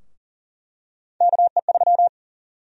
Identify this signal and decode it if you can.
Morse code, heard as a keyed tone, about 40 wpm, 700 Hz. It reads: KE3
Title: USA Call Sign Prefixes - Repeated 2x - 40wpm